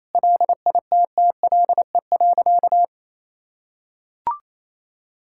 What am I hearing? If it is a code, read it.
Morse code, heard as a keyed tone, about 28 wpm, 700 Hz. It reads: LITTLE.
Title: Sentences from Top 1600 Words 28wpm